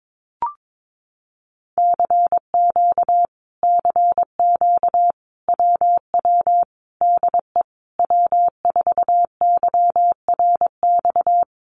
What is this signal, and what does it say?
Morse code, heard as a keyed tone, about 22 wpm, 700 Hz. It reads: CQ CQ WW DE W4YRX
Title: Common QSO Phrases Encore 22wpm